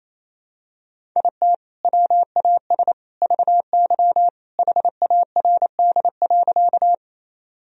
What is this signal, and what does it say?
Morse code, heard as a keyed tone, about 28 wpm, 700 Hz. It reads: IT WAS VY HARD.
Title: Sentences from Top 300 Words and Abbreviations 28wpm